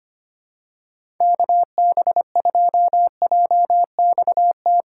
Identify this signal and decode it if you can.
Morse code, heard as a keyed tone, about 25 wpm, 700 Hz. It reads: KB2JXT
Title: USA Call Signs - 6x Character Spacing - 25wpm